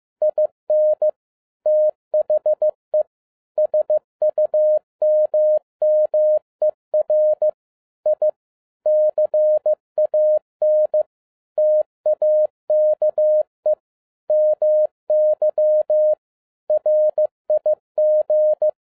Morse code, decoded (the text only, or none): IN THE SUMMER I CAN TAKE MY RIG